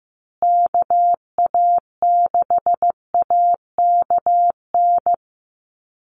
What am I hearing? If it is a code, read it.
Morse code, heard as a keyed tone, about 15 wpm, 700 Hz. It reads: KA6AKN
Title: USA Call Signs Repeated 3x 15wpm